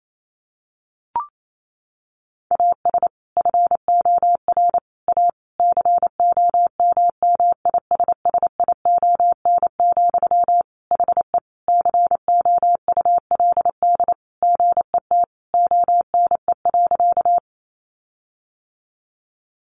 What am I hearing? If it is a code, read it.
Morse code, heard as a keyed tone, about 28 wpm, 700 Hz. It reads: AS FOR A COMMISSION, HE COULD GET ONE.
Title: Sentences from Top 1400 Words 28wpm